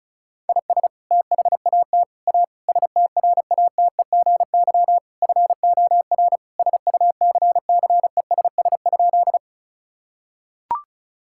Morse code, heard as a keyed tone, about 36 wpm, 700 Hz. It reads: IS THAT A STRATEGY FOR SUCCESS?
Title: Sentences from Top 1000 Words - Review Part 5 - 35wpm